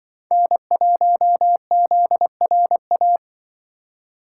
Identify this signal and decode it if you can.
Morse code, heard as a keyed tone, about 23 wpm, 700 Hz. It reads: N1ZRA
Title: Speed Racing - USA 1x3 Call Signs 20wpm